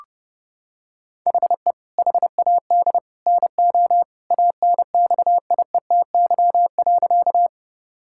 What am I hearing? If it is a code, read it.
Morse code, heard as a keyed tone, about 29 wpm, 700 Hz. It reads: HE HAD NO ANXIETY.
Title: Sentences from Top 1600 Words 30wpm